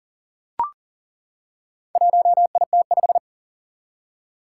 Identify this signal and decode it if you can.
Morse code, heard as a keyed tone, about 40 wpm, 700 Hz. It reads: 1IT5
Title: Four Challenging Letter-Number Combinations 40wpm